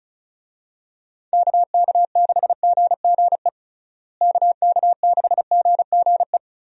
Morse code, heard as a keyed tone, about 34 wpm, 700 Hz. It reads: KK6GGE KK6GGE
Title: USA Call Signs - 6 Characters - Repeated 2x - 35wpm